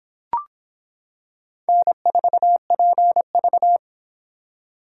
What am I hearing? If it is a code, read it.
Morse code, heard as a keyed tone, about 24 wpm, 700 Hz. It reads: N4PV